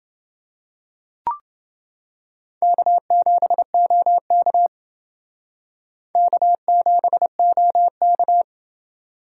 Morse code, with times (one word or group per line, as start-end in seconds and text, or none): 2.6-4.7 K7OK
6.2-8.4 K7OK